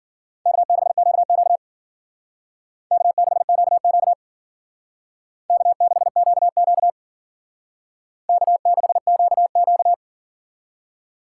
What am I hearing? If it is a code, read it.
Morse code, heard as a keyed tone, about 52 wpm, 700 Hz. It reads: K6QQ K6QQ K6QQ K6QQ